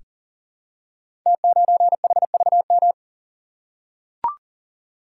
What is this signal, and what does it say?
Morse code, heard as a keyed tone, about 39 wpm, 700 Hz. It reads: T9HVM
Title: Code Groups - 5 Letter:Numbers - 40wpm